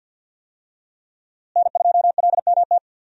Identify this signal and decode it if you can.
Morse code, heard as a keyed tone, about 47 wpm, 700 Hz. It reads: N2LMT
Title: USA Call Signs - 5 Characters - Repeated 3x - 50wpm